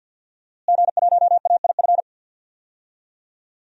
Morse code, heard as a keyed tone, about 53 wpm, 700 Hz. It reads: G1AIF
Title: DXCC-100 Call Signs - 5 Characters - Repeated 3x - 50wpm